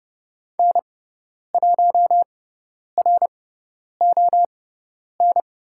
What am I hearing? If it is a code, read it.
Morse code, heard as a keyed tone, about 30 wpm, spaced as for 10 wpm, 700 Hz. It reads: N1RON
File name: Lesson 18g - Letter m - Call Signs 30wpm